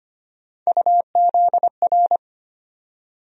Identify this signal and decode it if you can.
Morse code, heard as a keyed tone, about 25 wpm, 700 Hz. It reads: UZR
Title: USA Call Sign Postfixes - 3 Letters 25wpm